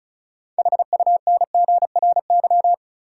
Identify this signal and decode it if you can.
Morse code, heard as a keyed tone, about 36 wpm, 700 Hz. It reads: HUNGRY
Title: Sentences from Top 1400 Words - New Word Drill Alt - 35wpm